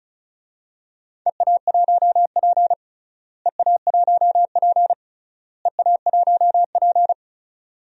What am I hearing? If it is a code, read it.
Morse code, heard as a keyed tone, about 34 wpm, 700 Hz. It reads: EA1P EA1P EA1P
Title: DXCC-100 Call Signs - 4 Characters - Repeated 3x - 35wpm